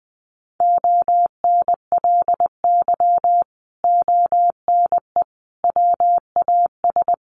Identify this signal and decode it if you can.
Morse code, heard as a keyed tone, about 20 wpm, 700 Hz. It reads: ONLY ONE WAS
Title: Sentences from Top 1400 Words 20wpm